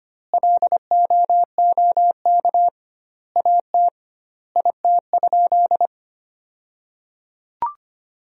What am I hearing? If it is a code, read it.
Morse code, heard as a keyed tone, about 25 wpm, 700 Hz. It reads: LOOK AT IT?
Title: 2x Word-Spacing - Sentences from Top 100 Words - 25wpm